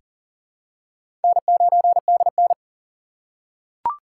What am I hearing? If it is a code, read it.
Morse code, heard as a keyed tone, about 40 wpm, 700 Hz. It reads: N9DN
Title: USA Call Signs - 4 Characters - 3x Character Spacing - 40wpm